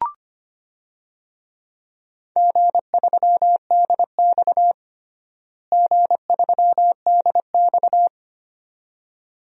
Morse code, heard as a keyed tone, about 25 wpm, 700 Hz. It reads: G3DX G3DX
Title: DXCC-100 Call Signs - 4 Characters - Repeated 2x - 25wpm